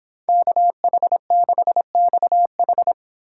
Morse code, heard as a keyed tone, about 26 wpm, 700 Hz. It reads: KH6XH